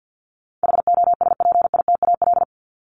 Morse code, heard as a keyed tone, about 50 wpm, 700 Hz. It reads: HOSPITAL
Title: Top 1751-2000 Words 50wpm